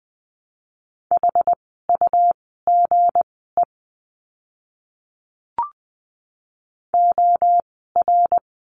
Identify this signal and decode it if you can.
Morse code, heard as a keyed tone, about 20 wpm, spaced as for 14 wpm, 700 Hz. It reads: HUGE OR
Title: 160 Lesson 26f - Letter g - Words 20wpm